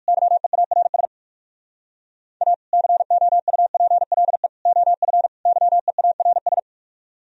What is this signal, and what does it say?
Morse code, heard as a keyed tone, about 56 wpm, 700 Hz. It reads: YEARS A COUPLE OF YEARS